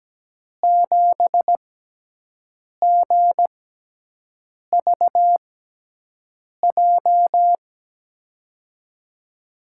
Morse code, heard as a keyed tone, about 17 wpm, spaced as for 6 wpm, 700 Hz. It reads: 7GVJ
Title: USA Call Signs - 6x Character Spacing - 17wpm